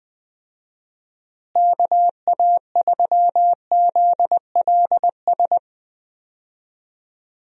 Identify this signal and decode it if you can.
Morse code, heard as a keyed tone, about 20 wpm, 700 Hz. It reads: KA3ZLS
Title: USA Call Signs - 6 Characters - Repeated 3x - 20wpm